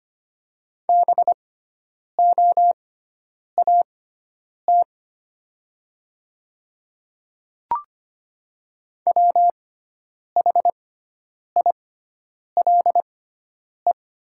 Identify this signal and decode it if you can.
Morse code, heard as a keyed tone, about 25 wpm, spaced as for 9 wpm, 700 Hz. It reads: BOAT WHILE
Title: 6x Character Spacing - Top 500 Words - 25wpm